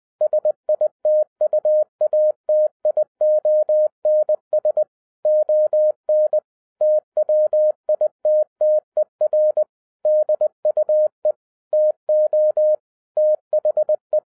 Morse code, heard as a keyed tone, about 20 wpm, 610 Hz. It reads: SITUATIONS ON TWITTER DUE TO THE
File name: 2011-07-mp3 / n33 THE 3.11 2 20wpm